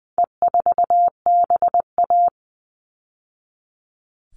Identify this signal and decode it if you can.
Morse code, heard as a keyed tone, about 20 wpm, 700 Hz. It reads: E4BA
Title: All Cut Numbers - 4-Digit 20wpm